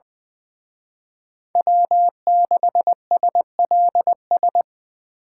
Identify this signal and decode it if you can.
Morse code, heard as a keyed tone, about 24 wpm, 700 Hz. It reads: W6SLS
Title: Speed Racing - USA 1x3 Call Signs 20wpm